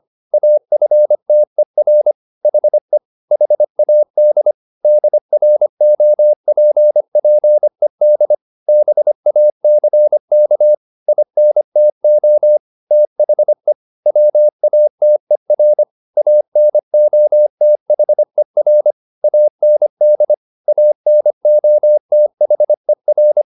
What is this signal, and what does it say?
Morse code, heard as a keyed tone, about 25 wpm, 600 Hz. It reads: AFTER HE HAD DROPPED BACK INTO THE WATER ANOTHER AND ANOTHER
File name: the old man and the sea part 1 25wpm